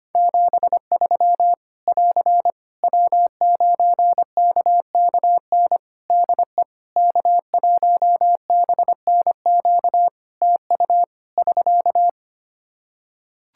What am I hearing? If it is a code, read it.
Morse code, heard as a keyed tone, about 25 wpm, 700 Hz. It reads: 73 <AR> W9KKN DE K1BNQ TU <SK>
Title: K1USN SST QSO Practice 25wpm